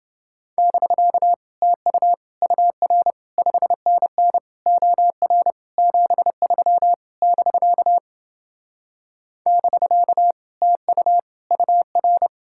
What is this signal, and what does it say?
Morse code, heard as a keyed tone, about 31 wpm, 700 Hz. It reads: <BK> TU UR 5NN OR 73 <BK> <BK> TU UR